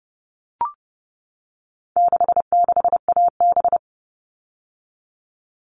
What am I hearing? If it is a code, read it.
Morse code, heard as a keyed tone, about 30 wpm, 700 Hz. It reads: 66AB
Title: All Cut Numbers - 4-Digit 30wpm